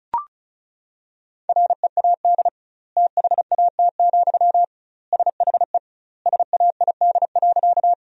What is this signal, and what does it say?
Morse code, heard as a keyed tone, about 36 wpm, 700 Hz. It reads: READ THAT, SHE SAID.
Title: 2x Word-Spacing - Sentences from Top 200 Words - 35wpm